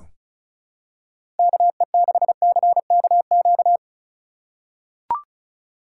Morse code, heard as a keyed tone, about 36 wpm, 700 Hz. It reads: KE6CKQ
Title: USA Call Signs - 6 Characters - 4x Character Spacing - 35wpm